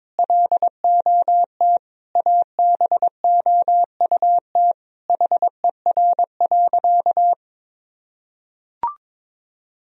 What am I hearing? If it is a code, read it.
Morse code, heard as a keyed tone, about 22 wpm, 700 Hz. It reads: LOT ABOUT HER.
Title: Sentences from Top 1000 Words - Review Part 3 - 22wpm